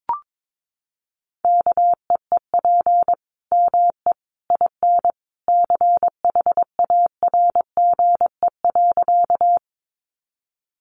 Morse code, heard as a keyed tone, about 22 wpm, 700 Hz. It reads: KEEP ME IN CHARGE.
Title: Sentences from Top 900 Words 22wpm